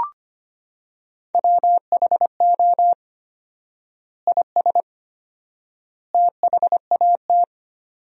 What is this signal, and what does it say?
Morse code, heard as a keyed tone, about 25 wpm, 700 Hz. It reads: WHO IS THAT
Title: 4x Word-Spacing - Sentences from Top 200 Words - 25wpm